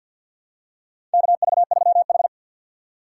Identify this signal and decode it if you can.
Morse code, heard as a keyed tone, about 51 wpm, 700 Hz. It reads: KV3H
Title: USA Call Signs - 2x Character Spacing - 50wpm